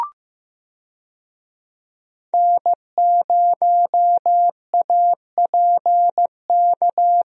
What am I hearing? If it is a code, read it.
Morse code, heard as a keyed tone, about 15 wpm, 700 Hz. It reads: N0APK
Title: USA Call Signs - 5 Characters - 15wpm